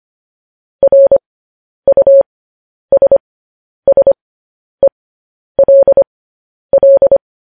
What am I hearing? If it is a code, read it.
Morse code, heard as a keyed tone, about 25 wpm, spaced as for 10 wpm, 550 Hz. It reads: RUSSELL
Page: news headlines in morse code at 10 wpm